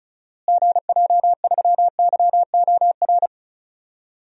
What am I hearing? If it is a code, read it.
Morse code, heard as a keyed tone, about 36 wpm, 700 Hz. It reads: GJ3YOR
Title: DXCC-100 Call Signs - 6 Characters - Repeated 3x - 35wpm